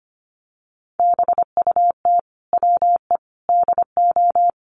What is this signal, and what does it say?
Morse code, heard as a keyed tone, about 25 wpm, 700 Hz. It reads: BUT WE DO
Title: Sets of 5 Words Encore2 25wpm